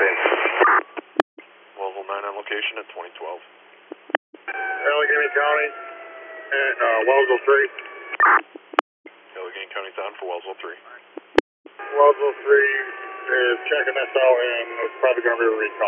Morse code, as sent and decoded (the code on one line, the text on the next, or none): none